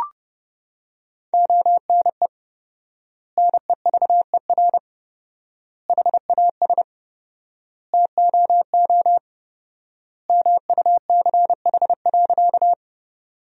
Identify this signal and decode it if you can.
Morse code, heard as a keyed tone, about 30 wpm, 700 Hz. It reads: ONE NEVER HAS TOO MUCH.
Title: 4x Word-Spacing - Sentences from Top 300 Words - 30wpm